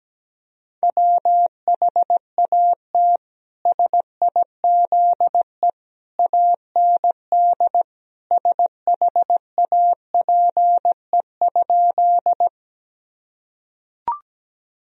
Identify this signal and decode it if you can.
Morse code, heard as a keyed tone, about 17 wpm, 700 Hz. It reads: WHAT SIZE AND SHAPE?
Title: Sentences from Top 1000 Words - Review Part 3 - 17wpm